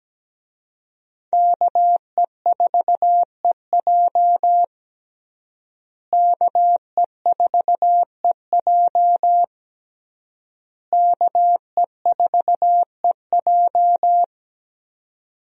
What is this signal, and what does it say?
Morse code, heard as a keyed tone, about 17 wpm, 700 Hz. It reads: KE4EJ KE4EJ KE4EJ
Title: USA Call Signs - 5 Characters - Repeated 3x - 17wpm